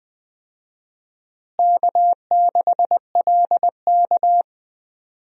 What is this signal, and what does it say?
Morse code, heard as a keyed tone, about 20 wpm, 700 Hz. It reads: K6LK
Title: USA Call Signs - 4 Characters - Repeated 3x - 20wpm